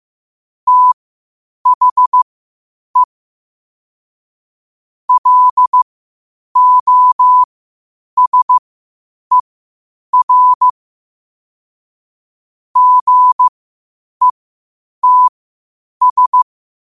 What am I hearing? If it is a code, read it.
Morse code, heard as a keyed tone, about 15 wpm, spaced as for 8 wpm, 1000 Hz. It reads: THE LOSER GETS
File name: qotdfiles / 8wpm